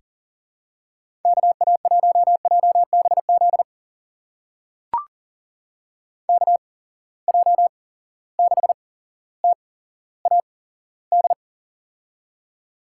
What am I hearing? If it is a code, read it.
Morse code, heard as a keyed tone, about 39 wpm, spaced as for 17 wpm, 700 Hz. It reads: KA1JBZ KJ6TAD
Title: USA Call Signs - 6 Characters - 8x Character Spacing - 40wpm